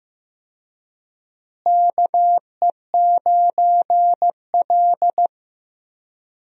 Morse code, heard as a keyed tone, about 15 wpm, 700 Hz. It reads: KE9L
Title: USA Call Signs - 4 Characters - Repeated 3x - 15wpm